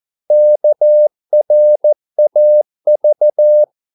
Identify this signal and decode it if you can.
Morse code, heard as a keyed tone, about 14 wpm, 600 Hz. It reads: KRAV
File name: Korte ord DK 14wpm